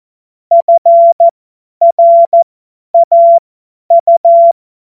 Morse code, heard as a keyed tone, about 14 wpm, spaced as for 10 wpm, 680 Hz. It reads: FRAU
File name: Morsekurs / Witze / Witz 04. Juli 2023 50 F70